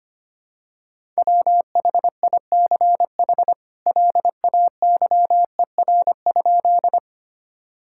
Morse code, heard as a keyed tone, about 25 wpm, 700 Hz. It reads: WHICH LAYER?
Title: Sentences from Top 1600 Words 25wpm